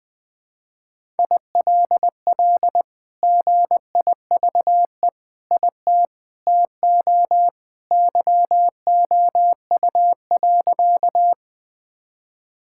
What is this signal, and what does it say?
Morse code, heard as a keyed tone, about 21 wpm, 700 Hz. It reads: ILL GIVE IT TO YOU.
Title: Sentences with 2-Word Phrases - 20wpm